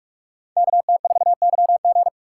Morse code, heard as a keyed tone, about 45 wpm, 700 Hz. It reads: KT4YG